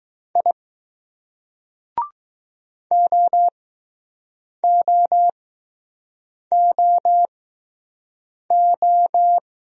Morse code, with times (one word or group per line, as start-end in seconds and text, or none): 0.4-0.5 I
2.9-9.4 OOOO